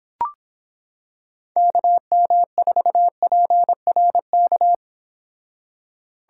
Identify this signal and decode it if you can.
Morse code, heard as a keyed tone, about 25 wpm, 700 Hz. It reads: KM4PRK